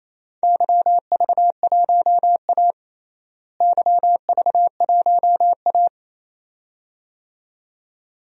Morse code, heard as a keyed tone, about 27 wpm, 700 Hz. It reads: YV1A YV1A
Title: DXCC-100 Call Signs - 4 Characters - Repeated 2x - 28wpm